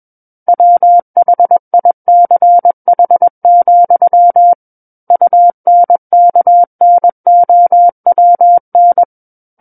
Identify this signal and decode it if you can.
Morse code, heard as a keyed tone, about 21 wpm, 700 Hz. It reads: WHICH, UNKNOWN
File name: Book One / 12-Chapter 12 - 21 WPM